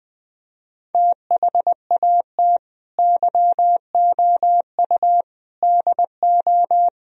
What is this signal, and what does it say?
Morse code, heard as a keyed tone, about 20 wpm, 700 Hz. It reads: THAT YOU DO